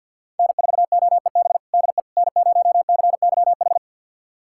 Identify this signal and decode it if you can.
Morse code, heard as a keyed tone, about 50 wpm, 700 Hz. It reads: N4OEB DE N0CYH
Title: USA Callsign DE Callsign 50wpm